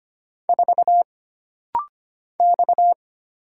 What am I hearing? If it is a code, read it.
Morse code, heard as a keyed tone, about 25 wpm, 700 Hz. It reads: 4 X
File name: Single Character (Letter:Number:Punctuation:Prosign) - Rapid-Fire - 25wpm